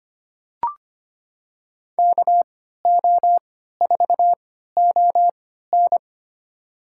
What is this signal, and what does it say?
Morse code, heard as a keyed tone, about 25 wpm, spaced as for 14 wpm, 700 Hz. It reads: KO4ON
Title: USA Call Signs - 5 Characters - 3x Character Spacing - 25wpm